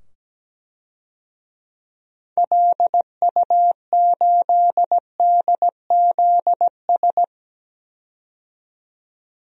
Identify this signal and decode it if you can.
Morse code, heard as a keyed tone, about 17 wpm, 700 Hz. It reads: LU8DZS
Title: DXCC-100 Call Signs - 6 Characters - 17wpm